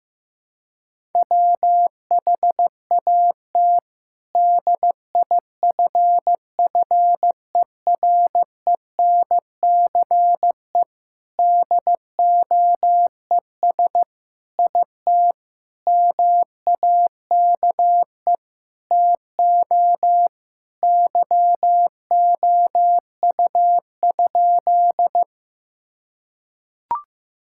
Morse code, heard as a keyed tone, about 15 wpm, 700 Hz. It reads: WHAT DIFFERENCE DOES IT MAKE TO YOU?
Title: Sentences from Top 1000 Words - Review Part 5 - 15wpm